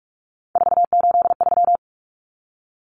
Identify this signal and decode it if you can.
Morse code, heard as a keyed tone, about 44 wpm, 700 Hz. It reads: 483
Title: Numbers - 3-Digit 45wpm